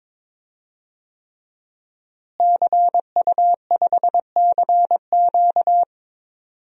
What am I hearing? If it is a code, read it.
Morse code, heard as a keyed tone, about 22 wpm, 700 Hz. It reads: CU5CQ
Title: DXCC-100 Call Signs - 5 Characters - Repeated 3x - 22wpm